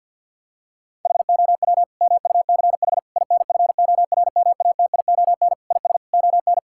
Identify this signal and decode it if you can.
Morse code, heard as a keyed tone, about 50 wpm, 700 Hz. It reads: HOW MUCH INFORMATION IS ON
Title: Sentences from Top 1000 Words - Review Part 6 - 50wpm